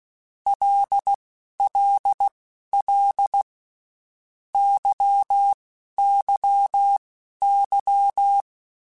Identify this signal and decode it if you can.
Morse code, heard as a keyed tone, about 16 wpm, spaced as for 11 wpm, 800 Hz. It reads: LLL YYY
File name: Groups@10 / Groups@10 mp3s / LFCYQ@10